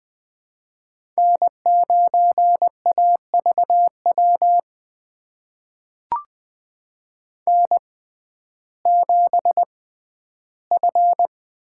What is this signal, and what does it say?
Morse code, heard as a keyed tone, about 20 wpm, spaced as for 12 wpm, 700 Hz. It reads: N9AVW N7F
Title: USA Call Signs - 5 Characters - 6x Character Spacing - 20wpm